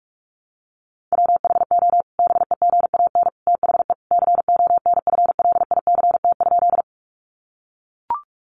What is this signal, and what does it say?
Morse code, heard as a keyed tone, about 44 wpm, 700 Hz. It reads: WHO BEGAN THE CONFLICT?
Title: Sentences from Top 1300 Words 45wpm